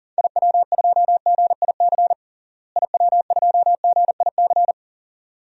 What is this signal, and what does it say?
Morse code, heard as a keyed tone, about 40 wpm, 700 Hz. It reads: IW2GIC IW2GIC
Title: DXCC-100 Call Signs - 6 Characters - Repeated 2x - 40wpm